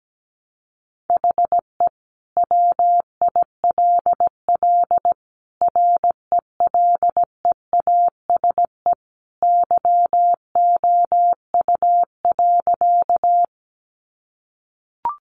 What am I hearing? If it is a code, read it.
Morse code, heard as a keyed tone, about 17 wpm, 700 Hz. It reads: HE WILL RELEASE YOU.
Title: Sentences from Top 1100 Words 17wpm